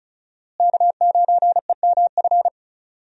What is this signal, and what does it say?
Morse code, heard as a keyed tone, about 34 wpm, 700 Hz. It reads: K9EMF
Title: USA Call Signs - 5 Characters - Repeated 2x - 35wpm